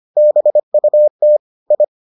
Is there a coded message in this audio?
Yes: Morse code from 0.2 to 1.8 s, heard as a keyed tone, about 25 wpm, 600 Hz. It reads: BUT I